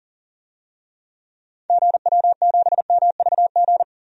Morse code, heard as a keyed tone, about 40 wpm, 700 Hz. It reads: GW7MVG
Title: DXCC-100 Call Signs - 6 Characters - 40wpm